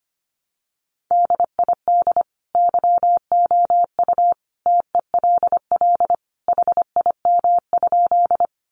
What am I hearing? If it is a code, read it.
Morse code, heard as a keyed tone, about 25 wpm, 700 Hz. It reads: DID YOU TELL HIM?
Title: Sentences from Top 200 Words 25wpm